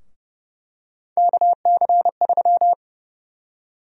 Morse code, heard as a keyed tone, about 31 wpm, 700 Hz. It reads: KC3
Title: USA Call Sign Prefixes - 30wpm